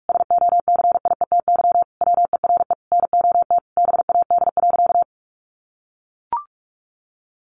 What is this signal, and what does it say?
Morse code, heard as a keyed tone, about 44 wpm, 700 Hz. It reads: SOCIETY WERE NOT BAD.
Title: Sentences from Top 900 Words 45wpm